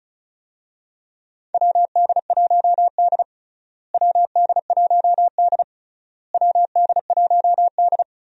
Morse code, heard as a keyed tone, about 36 wpm, 700 Hz. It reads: WD1D WD1D WD1D
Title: USA Call Signs - 4 Characters - Repeated 3x - 35wpm